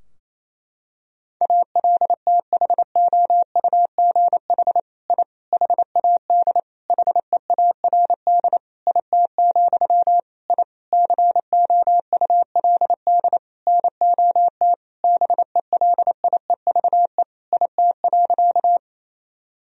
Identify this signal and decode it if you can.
Morse code, heard as a keyed tone, about 28 wpm, 700 Hz. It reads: ALTHOUGH I HAD HEARD IT, I COULD NOT BELIEVE IT.
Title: Sentences from Top 1000 Words - Review Part 3 - 28wpm